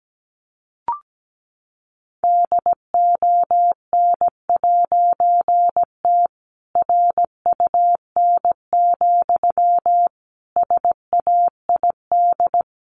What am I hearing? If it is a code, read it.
Morse code, heard as a keyed tone, about 17 wpm, 700 Hz. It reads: DON'T RUN, SAID
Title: A Sound Of Thunder 17wpm